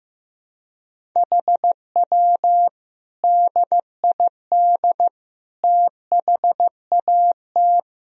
Morse code, heard as a keyed tone, about 16 wpm, 700 Hz. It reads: HW DID THAT
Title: Sentences from Top 200 Words and Abbreviations 15wpm